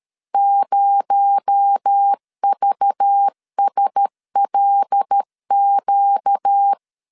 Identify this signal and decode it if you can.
Morse code, heard as a keyed tone, about 13 wpm, 800 Hz. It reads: 0VSLQ
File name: LEZIONI / 6 PKX0 / 6 PKX0 12wpm 2231